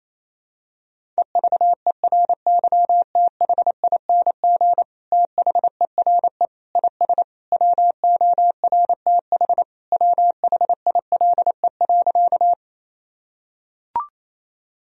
Morse code, heard as a keyed tone, about 28 wpm, 700 Hz. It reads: EVERYTHING THERE IS WORTH WHILE.
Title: Sentences from Top 1000 Words - Review Part 1 - 28wpm